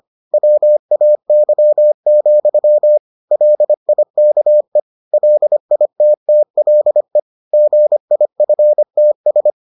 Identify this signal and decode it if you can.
Morse code, heard as a keyed tone, about 25 wpm, 600 Hz. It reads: WAY, LIKE LITTLE GIFTS